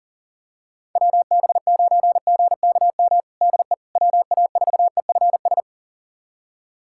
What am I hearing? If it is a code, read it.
Morse code, heard as a keyed tone, about 37 wpm, 690 Hz. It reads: WB9GKM DE WA4EFS
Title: USA Callsign DE Callsign - 6 Characters - 40wpm